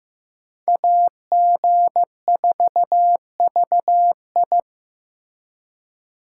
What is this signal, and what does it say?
Morse code, heard as a keyed tone, about 15 wpm, 700 Hz. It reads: AG4VI
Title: USA Call Signs - 5 Characters - 2x Character Spacing - 15wpm